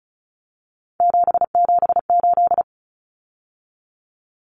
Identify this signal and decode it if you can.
Morse code, heard as a keyed tone, about 34 wpm, 700 Hz. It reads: 778
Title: Numbers - 3-Digit 35wpm